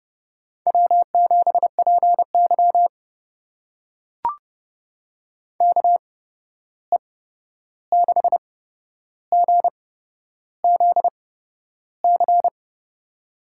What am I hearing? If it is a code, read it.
Morse code, heard as a keyed tone, about 30 wpm, spaced as for 11 wpm, 700 Hz. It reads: W7PY KE6GZC
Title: USA Call Signs - 8x Character Spacing - 30wpm